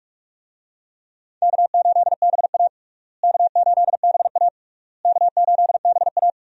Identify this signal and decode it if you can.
Morse code, heard as a keyed tone, about 43 wpm, 700 Hz. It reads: K8BA K8BA K8BA
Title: USA Call Signs - 4 Characters - Repeated 3x - 45wpm